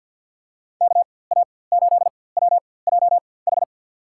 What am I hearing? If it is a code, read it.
Morse code, heard as a keyed tone, about 53 wpm, spaced as for 24 wpm, 700 Hz. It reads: KA8WJH
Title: USA Call Signs - 4x Character Spacing - 50wpm